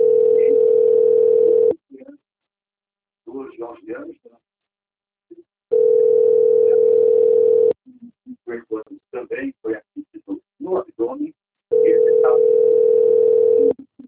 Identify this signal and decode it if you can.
Ringback tone (North America), 440 + 480 Hz, with speech.
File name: rg-1600-unknown- - - .34294